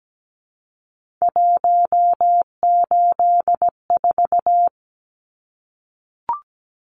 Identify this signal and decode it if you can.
Morse code, heard as a keyed tone, about 17 wpm, 700 Hz. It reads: 184
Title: Numbers - 3-Digit 17wpm